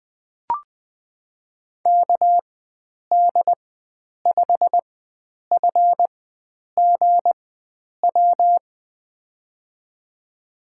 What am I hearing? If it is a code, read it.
Morse code, heard as a keyed tone, about 20 wpm, spaced as for 9 wpm, 700 Hz. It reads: KD5FGW
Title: USA Call Signs - 4x Character Spacing - 20wpm